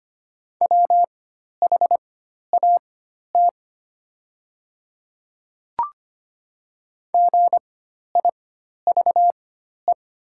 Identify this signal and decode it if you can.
Morse code, heard as a keyed tone, about 25 wpm, spaced as for 12 wpm, 700 Hz. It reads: WHAT GIVE
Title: 4x Character Spacing - Top 300 Words - 25wpm